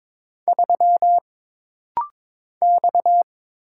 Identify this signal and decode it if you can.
Morse code, heard as a keyed tone, about 22 wpm, 700 Hz. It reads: 3 X
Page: This rapid-fire timing is geared to push your ICR (Instant Character Recognition) skills.